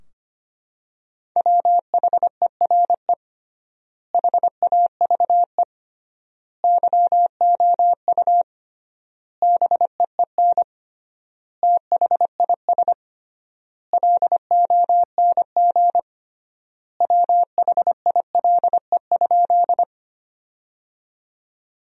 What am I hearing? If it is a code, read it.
Morse code, heard as a keyed tone, about 25 wpm, 700 Hz. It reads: WHERE HAVE YOU BEEN THIS LONG WHILE?
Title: 3x Word-Spacing - Sentences from Top 300 Words - 25wpm